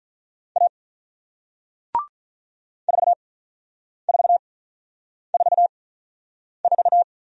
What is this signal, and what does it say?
Morse code, heard as a keyed tone, about 45 wpm, spaced as for 9 wpm, 700 Hz. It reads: A 4444